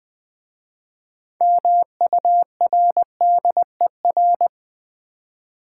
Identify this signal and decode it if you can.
Morse code, heard as a keyed tone, about 20 wpm, 700 Hz. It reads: MURDER
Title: 6 Letter Words - First 3 Letters Given - 20wpm